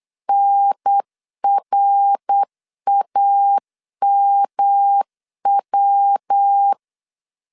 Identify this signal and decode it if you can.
Morse code, heard as a keyed tone, about 8 wpm, 800 Hz. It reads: NRAMW
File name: CW / DET / 1 3 1500C S B 8wpm 4141